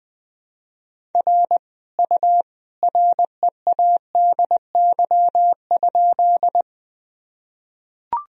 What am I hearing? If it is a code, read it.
Morse code, heard as a keyed tone, about 20 wpm, 700 Hz. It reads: R U READY?
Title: Sentences from Top 400 Words and Abbreviations 20wpm